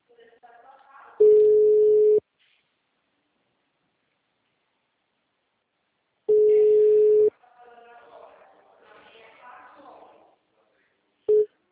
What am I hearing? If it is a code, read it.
Ringback tone (Europe, 425 Hz), with speech.